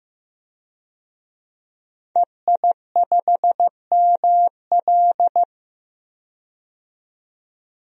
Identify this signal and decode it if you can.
Morse code, heard as a keyed tone, about 16 wpm, 700 Hz. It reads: EI5ML
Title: DXCC-100 Call Signs - 5 Characters - 15wpm